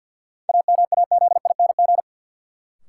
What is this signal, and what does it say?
Morse code, heard as a keyed tone, about 53 wpm, 700 Hz. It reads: AMAZING
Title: Sentences from Top 1500 Words - New Word Drill Alt - 50wpm